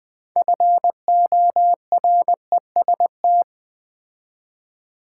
Morse code, heard as a keyed tone, about 20 wpm, 700 Hz. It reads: FOREST
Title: Build A Word - 3 Letters At A Time - 6-Letter Words - 20wpm